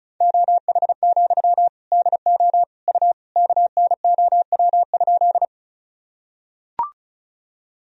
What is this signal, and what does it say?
Morse code, heard as a keyed tone, about 34 wpm, 700 Hz. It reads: OH, DO U KNOW?
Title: Sentences from Top 500 Words and Abbreviations 35wpm